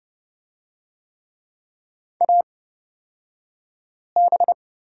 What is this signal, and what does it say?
Morse code, heard as a keyed tone, about 30 wpm, spaced as for 5 wpm, 700 Hz. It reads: AB
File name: Lesson 38d - Letter q - Two Characters 30wpm